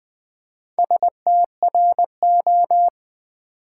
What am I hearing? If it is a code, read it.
Morse code, heard as a keyed tone, about 20 wpm, 700 Hz. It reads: STRO
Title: Build A Word - 2 Letters At A Time - 8-Letter Words - 20wpm